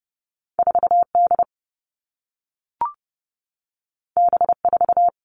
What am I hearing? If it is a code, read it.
Morse code, heard as a keyed tone, about 30 wpm, 700 Hz. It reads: 4D B4
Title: All Cut Numbers - 2-Digit 30wpm